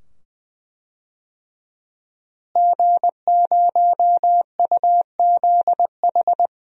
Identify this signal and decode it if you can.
Morse code, heard as a keyed tone, about 20 wpm, 700 Hz. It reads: G0UZH
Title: DXCC-100 Call Signs - 5 Characters - Repeated 2x - 20wpm